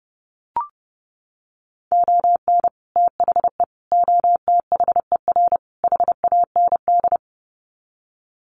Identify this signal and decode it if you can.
Morse code, heard as a keyed tone, about 30 wpm, 700 Hz. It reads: ON THE OTHER HAND
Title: Sets of 4 Words 30wpm